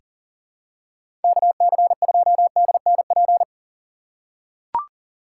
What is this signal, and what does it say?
Morse code, heard as a keyed tone, about 40 wpm, 700 Hz. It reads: KC2DNP
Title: USA Call Signs - 2x Character Spacing - 40wpm